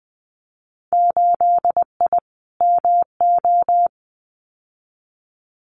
Morse code, heard as a keyed tone, about 20 wpm, 700 Hz. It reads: 8I MO
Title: Winter Field Day Exchanges Encore 20wpm